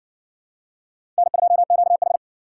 Morse code, heard as a keyed tone, about 52 wpm, 700 Hz. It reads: N2QH